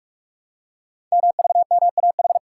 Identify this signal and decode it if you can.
Morse code, heard as a keyed tone, about 44 wpm, 700 Hz. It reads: MVMAH
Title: Code Groups - 5 Letters - 45wpm